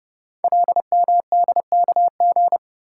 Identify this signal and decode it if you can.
Morse code, heard as a keyed tone, about 30 wpm, 700 Hz. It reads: LMDKG
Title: Code Groups - 5 Letters - 30wpm